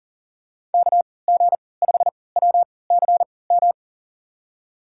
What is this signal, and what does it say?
Morse code, heard as a keyed tone, about 40 wpm, spaced as for 23 wpm, 700 Hz. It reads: KG5WCM
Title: USA Call Signs - 3x Character Spacing - 40wpm